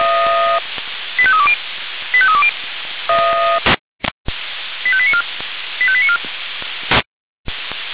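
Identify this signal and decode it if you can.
Five-tone selective call (ZVEI), heard as quick runs of five tones, tones of about 70 ms. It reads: 85320 85320 85894 85894
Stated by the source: ZVEI five-tone code 85310 received by SDR at fire brigade Feuerwehr Orlamünde.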